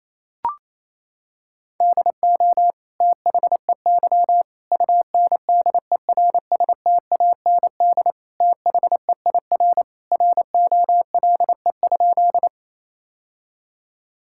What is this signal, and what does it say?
Morse code, heard as a keyed tone, about 28 wpm, 700 Hz. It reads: DO THEY UNDERSTAND THEIR ROLE?
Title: Sentences from Top 1000 Words - Review Part 6 - 28wpm